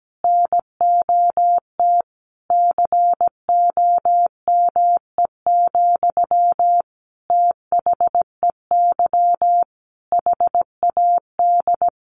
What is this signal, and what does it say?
Morse code, heard as a keyed tone, about 17 wpm, 700 Hz. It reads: NOT COME, THEY HAD
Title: Sentences from Top 1300 Words 17wpm